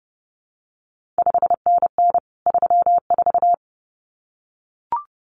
Morse code, heard as a keyed tone, about 30 wpm, 700 Hz. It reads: 5NN 34
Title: CQ WW DX Contest Exchange Encore - 30wpm